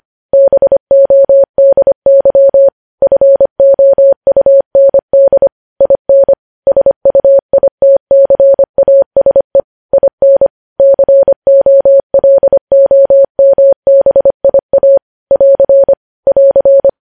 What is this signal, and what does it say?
Morse code, heard as a keyed tone, about 25 wpm, 550 Hz. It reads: BODY FOUND IN SUITCASE IN COLOMBIA <AR> <AR>
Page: news-headlines-in-morse-code-at-25-wpm